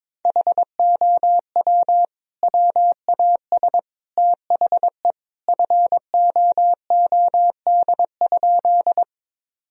Morse code, heard as a keyed tone, about 22 wpm, 700 Hz. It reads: HOW WAS THE FOOD?
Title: Sentences from Top 1000 Words - Review Part 1 - 22wpm